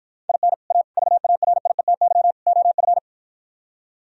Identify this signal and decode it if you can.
Morse code, heard as a keyed tone, about 53 wpm, 700 Hz. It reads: IN A VARIETY OF